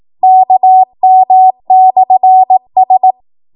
Morse code, heard as a keyed tone, about 18 wpm, 750 Hz. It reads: KM/S
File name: Morse / 18WPM